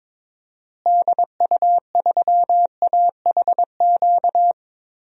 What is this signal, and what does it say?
Morse code, heard as a keyed tone, about 22 wpm, 700 Hz. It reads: DU3AHQ
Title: DXCC-100 Call Signs - 6 Characters - Repeated 2x - 22wpm